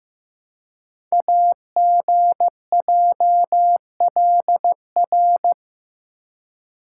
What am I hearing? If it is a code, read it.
Morse code, heard as a keyed tone, about 15 wpm, 700 Hz. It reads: AGJLR
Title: Code Groups - 5 Letters - 15wpm